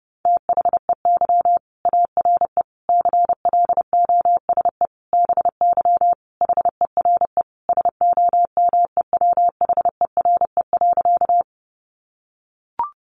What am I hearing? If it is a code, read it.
Morse code, heard as a keyed tone, about 30 wpm, 700 Hz. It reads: THEY ARE CLOSE BY HERE SOMEWHERE.
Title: Sentences from Top 1400 Words 30wpm